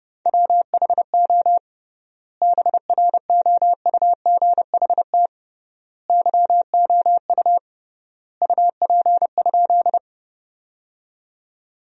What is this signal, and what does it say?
Morse code, heard as a keyed tone, about 30 wpm, 700 Hz. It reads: WHO BROUGHT YOU UP?
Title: 3x Word-Spacing - Sentences from Top 500 Words - 30wpm